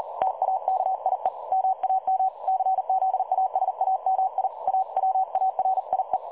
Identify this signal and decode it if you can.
Morse code, heard as a keyed tone, about 38 wpm, 760 Hz. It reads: TAKIE MAM CZASAMI AWARIE
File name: dlazaawansowanych / nagraniazpasma / websdr recording 3565.0kHz